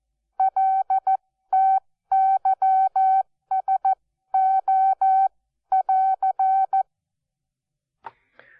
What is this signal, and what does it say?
Morse code, heard as a keyed tone, about 14 wpm, spaced as for 12 wpm, 770 Hz. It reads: LTYSO<AR>
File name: Onlinelehrgang / morsen / DJ4UF Morsen12